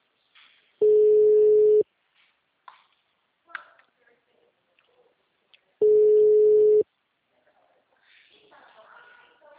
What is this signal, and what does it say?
Ringback tone (Europe, 425 Hz).